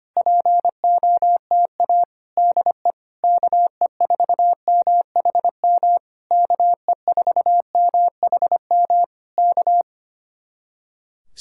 Morse code, heard as a keyed tone, about 25 wpm, 700 Hz. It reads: POTA DE KE4MHM KE4MHM K
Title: Speed-Racing Common QSO Phrases Encore - 25wpm